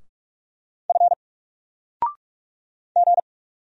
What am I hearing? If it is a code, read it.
Morse code, heard as a keyed tone, about 44 wpm, 700 Hz. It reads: F G
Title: Single Letter-Number 45wpm